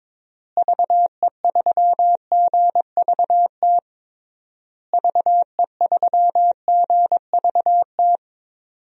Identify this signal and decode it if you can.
Morse code, heard as a keyed tone, about 22 wpm, 700 Hz. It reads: VE3GVT VE3GVT
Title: DXCC-100 Call Signs - 6 Characters - Repeated 3x - 22wpm